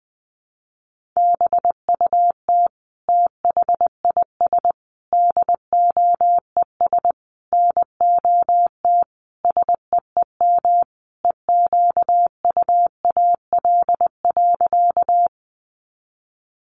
Morse code, heard as a keyed tone, about 20 wpm, 700 Hz. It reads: BUT THIS DOES NOT SEEM EQUAL.
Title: Sentences from Top 1400 Words 20wpm